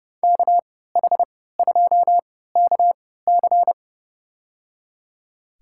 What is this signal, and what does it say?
Morse code, heard as a keyed tone, about 30 wpm, spaced as for 17 wpm, 700 Hz. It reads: KH2KC
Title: USA Call Signs - 5 Characters - 3x Character Spacing - 30wpm